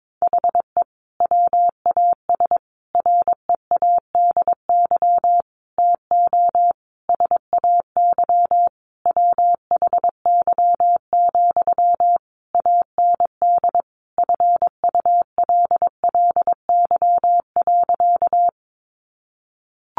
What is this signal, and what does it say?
Morse code, heard as a keyed tone, about 22 wpm, 700 Hz. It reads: HE WAS READY TO SAY WHY, AND FULLY.
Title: Sentences from Top 1300 Words 22wpm